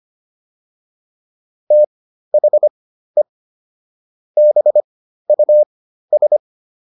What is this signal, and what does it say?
Morse code, heard as a keyed tone, about 25 wpm, spaced as for 13 wpm, 600 Hz. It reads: THE BUS